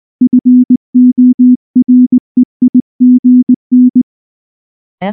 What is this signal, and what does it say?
Morse code, heard as a keyed tone, about 20 wpm, 250 Hz. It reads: FOREIGN
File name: morse / morse-training-20WPM-9